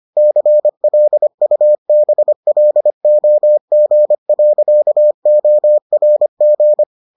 Morse code, heard as a keyed tone, about 25 wpm, 600 Hz. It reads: CLUBLOG.ORG